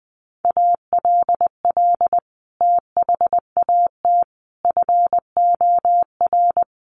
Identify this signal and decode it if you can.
Morse code, heard as a keyed tone, about 20 wpm, 700 Hz. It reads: ALL THAT FOR
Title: Sentences from Top 500 Words 20wpm